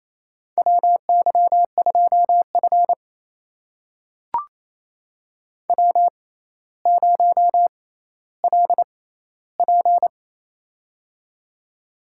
Morse code, heard as a keyed tone, about 28 wpm, spaced as for 14 wpm, 700 Hz. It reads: WY2F W0LP
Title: USA Call Signs - 6x Character Spacing - 28wpm